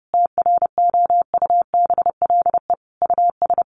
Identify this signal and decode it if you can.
Morse code, heard as a keyed tone, about 28 wpm, 700 Hz. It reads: TROUBLE US
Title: Sentences from Top 1400 Words 30wpm